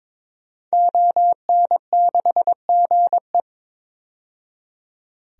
Morse code, heard as a keyed tone, about 22 wpm, 700 Hz. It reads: ON6GE
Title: DXCC-100 Call Signs - 5 Characters - 22wpm